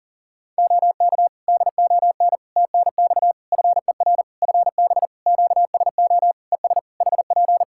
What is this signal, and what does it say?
Morse code, heard as a keyed tone, about 40 wpm, 700 Hz. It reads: OK DON TNX FER FB QSO ES HP
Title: Practice QSO - Exchange 4 - Ending - 40wpm